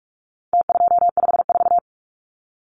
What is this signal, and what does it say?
Morse code, heard as a keyed tone, about 44 wpm, 700 Hz. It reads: T254
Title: Numbers - Contest Style - 4-Digit 45wpm